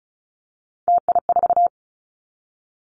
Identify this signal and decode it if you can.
Morse code, heard as a keyed tone, about 34 wpm, 700 Hz. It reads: TI4
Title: Callsign Prefixes for Top 100 DXCC - 35wpm